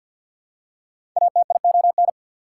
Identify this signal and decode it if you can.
Morse code, heard as a keyed tone, about 53 wpm, 690 Hz. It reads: ATION
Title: Prefixes and Suffixes 50wpm